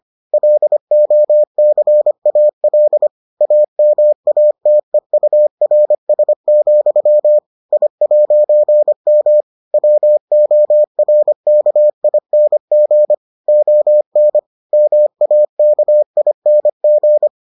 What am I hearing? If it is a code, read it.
Morse code, heard as a keyed tone, about 25 wpm, 600 Hz. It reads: LOCAL AMATEURS, I'M WORKING ON MAKING